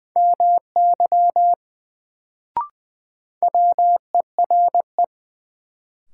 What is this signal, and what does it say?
Morse code, heard as a keyed tone, about 20 wpm, 700 Hz. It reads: MY WERE
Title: Sets of 2 Words - New Word Drill - 20wpm